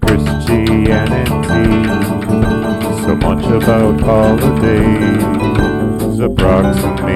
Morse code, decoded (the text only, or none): none